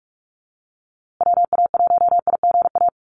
Morse code, heard as a keyed tone, about 44 wpm, 700 Hz. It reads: WA1IGA
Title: USA Call Signs Repeated 3x 45wpm